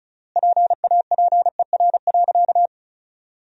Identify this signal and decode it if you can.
Morse code, heard as a keyed tone, about 36 wpm, 700 Hz. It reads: PAPER.
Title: 3x Word-Spacing - Sentences from Top 300 Words - 35wpm